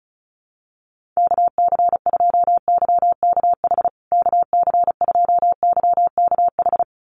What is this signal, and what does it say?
Morse code, heard as a keyed tone, about 34 wpm, 700 Hz. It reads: KC2YKH KC2YKH
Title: USA Call Signs Repeated 2x 35wpm